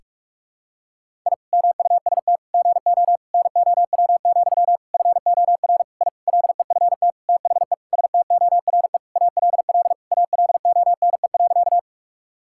Morse code, heard as a keyed tone, about 43 wpm, 700 Hz. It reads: I MUST GO NOW, FOR I LEFT THE STORE ALL ALONE.
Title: Sentences from Top 1000 Words - Review Part 5 - 45wpm